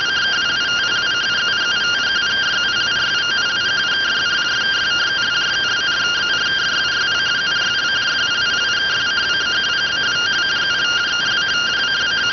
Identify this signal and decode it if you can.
Two-tone teleprinter signal, about 170 Hz shift.